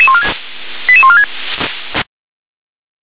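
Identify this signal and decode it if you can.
Five-tone selective call (ZVEI), heard as quick runs of five tones, tones of about 70 ms. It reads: 88146
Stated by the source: ZVEI five-tone call received by SDR.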